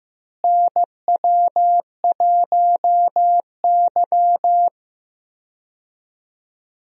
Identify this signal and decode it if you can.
Morse code, heard as a keyed tone, about 15 wpm, 700 Hz. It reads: NW1Y
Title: USA Call Signs - 4 Characters - Repeated 2x - 15wpm